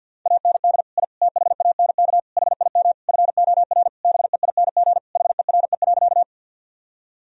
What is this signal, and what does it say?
Morse code, heard as a keyed tone, about 50 wpm, 700 Hz. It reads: AND I THANK HIM FOR BEING HERE.
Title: Sentences from Top 1000 Words - Review Part 3 - 50wpm